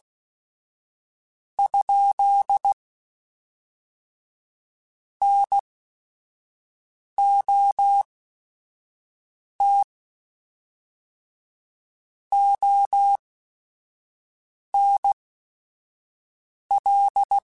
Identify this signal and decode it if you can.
Morse code, heard as a keyed tone, about 16 wpm, spaced as for 5 wpm, 800 Hz. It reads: ? NOT ONL
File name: Technical Subjects / Technical Subjects-mp3s / AND Gate @5